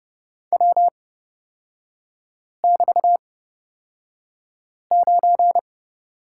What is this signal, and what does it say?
Morse code, heard as a keyed tone, about 30 wpm, spaced as for 5 wpm, 700 Hz. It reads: W<BT>9
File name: Lesson 36e - BT - Three Characters 30wpm